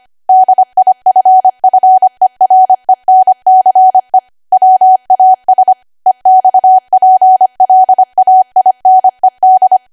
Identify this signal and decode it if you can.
Morse code, heard as a keyed tone, about 25 wpm, 750 Hz. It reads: DIFFERENCE WAS EXPLAINED
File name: Archive / 25 WPM / 25WPM